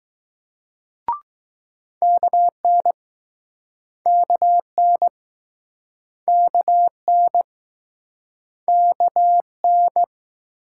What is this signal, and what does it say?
Morse code, heard as a keyed tone, about 21 wpm, 700 Hz. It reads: KN KN KN KN